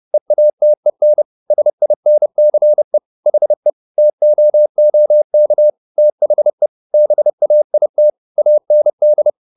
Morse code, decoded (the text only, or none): EATEN SINCE HE TOOK THE BAIT AND